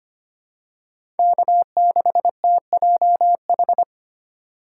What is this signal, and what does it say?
Morse code, heard as a keyed tone, about 25 wpm, 700 Hz. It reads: K6TJH
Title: USA Call Signs - 5 Characters - 4x Character Spacing - 25wpm